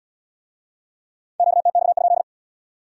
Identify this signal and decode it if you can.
Morse code, heard as a keyed tone, about 64 wpm, 700 Hz. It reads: QTX?